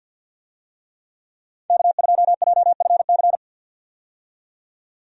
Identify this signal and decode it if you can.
Morse code, heard as a keyed tone, about 47 wpm, 700 Hz. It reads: K2JFC
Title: USA Call Signs - 5 Characters - 50wpm